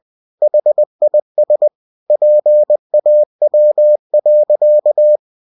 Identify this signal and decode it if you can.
Morse code, heard as a keyed tone, about 20 wpm, 600 Hz. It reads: HIS PAW.